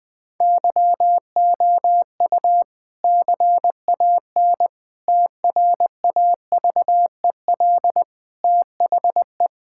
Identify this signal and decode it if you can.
Morse code, heard as a keyed tone, about 20 wpm, 700 Hz. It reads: YOU CAN TRAVEL THE
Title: Difficult Sentences with Hints - 20wpm